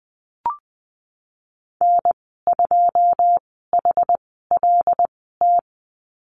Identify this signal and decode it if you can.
Morse code, heard as a keyed tone, about 20 wpm, spaced as for 15 wpm, 700 Hz. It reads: N2HLT
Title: 147 Lesson 24g - Letter y - Call Signs 20wpm